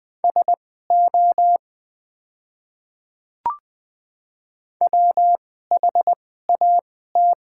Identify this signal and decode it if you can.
Morse code, heard as a keyed tone, about 20 wpm, spaced as for 14 wpm, 700 Hz. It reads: SO WHAT
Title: 2x Character Spacing - Top 200 Words - 20wpm